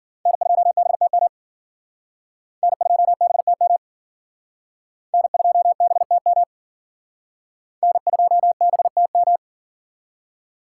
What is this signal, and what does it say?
Morse code, heard as a keyed tone, about 52 wpm, 700 Hz. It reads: N2BTM N2BTM N2BTM N2BTM